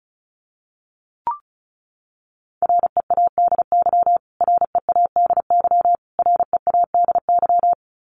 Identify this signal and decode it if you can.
Morse code, heard as a keyed tone, about 34 wpm, 700 Hz. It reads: READY READY READY
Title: Top 500 Words Repeated 35wpm